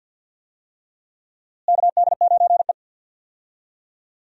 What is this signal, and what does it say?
Morse code, heard as a keyed tone, about 47 wpm, 700 Hz. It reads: KD9E
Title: USA Call Signs - 4 Characters - Repeated 3x - 50wpm